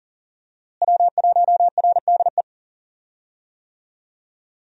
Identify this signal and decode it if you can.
Morse code, heard as a keyed tone, about 40 wpm, 700 Hz. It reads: W1RDE
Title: USA Call Signs - 5 Characters - Repeated 2x - 40wpm